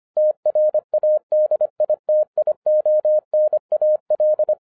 Morse code, heard as a keyed tone, about 25 wpm, 610 Hz. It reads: TRADITIONAL
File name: 2011-10-mp3 / n14-MORSE EXPRESS 2011 CHRISTMAS KEY 25wpm